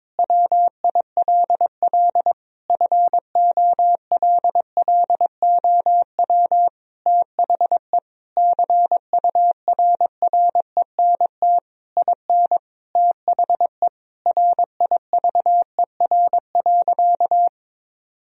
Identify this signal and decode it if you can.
Morse code, heard as a keyed tone, about 22 wpm, 700 Hz. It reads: WILL FOLLOW THE CURRENT IN THE RIVER.
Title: Sentences from Top 1000 Words - Review Part 4 - 22wpm